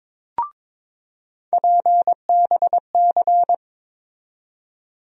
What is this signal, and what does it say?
Morse code, heard as a keyed tone, about 22 wpm, 700 Hz. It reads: PBC
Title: USA Call Sign Postfixes - 3 Letters 22wpm